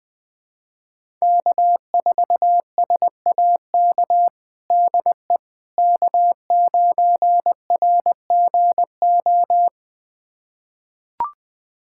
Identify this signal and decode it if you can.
Morse code, heard as a keyed tone, about 21 wpm, 700 Hz. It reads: K4SAK DE K9RGO
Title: USA Callsign DE Callsign - 5 Characters - 20wpm